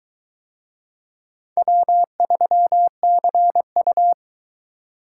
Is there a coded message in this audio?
Yes: Morse code from 1.6 to 4.1 s, heard as a keyed tone, about 22 wpm, 700 Hz. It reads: W3CU